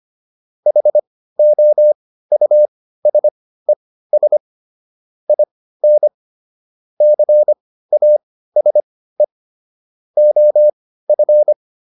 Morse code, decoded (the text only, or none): HOUSES IN CASE OF